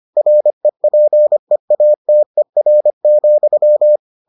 Morse code, heard as a keyed tone, about 25 wpm, 600 Hz. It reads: REPEATER,